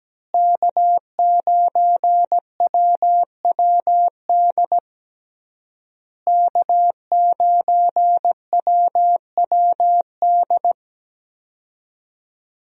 Morse code, heard as a keyed tone, about 17 wpm, 700 Hz. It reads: K9WWD K9WWD
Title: USA Call Signs - 5 Characters - Repeated 3x - 17wpm